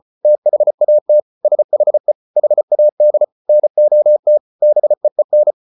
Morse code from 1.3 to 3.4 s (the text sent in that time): SHE HAD